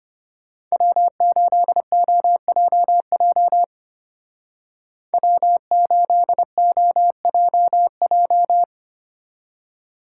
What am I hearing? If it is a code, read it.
Morse code, heard as a keyed tone, about 33 wpm, 700 Hz. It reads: W8OJJ W8OJJ